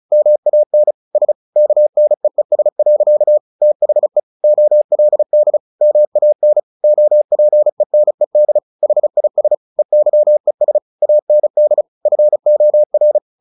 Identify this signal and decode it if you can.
Morse code, heard as a keyed tone, about 34 wpm, 600 Hz. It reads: MAN S KNEES. THE OLD MAN OPENED HIS EYES AND FOR
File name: the old man and the sea / the old man and the sea 1 35wpm